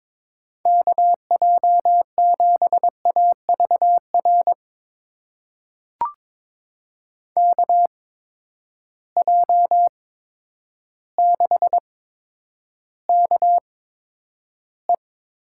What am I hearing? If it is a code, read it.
Morse code, heard as a keyed tone, about 22 wpm, spaced as for 10 wpm, 700 Hz. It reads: KJ7AVR KJ6KE
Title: USA Call Signs - 6 Characters - 8x Character Spacing - 22wpm